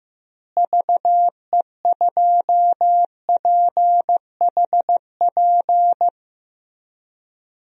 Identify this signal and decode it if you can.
Morse code, heard as a keyed tone, about 15 wpm, 700 Hz. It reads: VE2PHP
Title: DXCC-100 Call Signs - 6 Characters - Repeated 2x - 15wpm